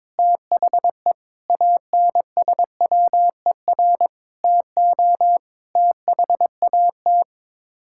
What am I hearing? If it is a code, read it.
Morse code, heard as a keyed tone, about 22 wpm, 700 Hz. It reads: THE ANSWER TO THAT
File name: practice / 22wpm / Sets of 7 Words - 22wpm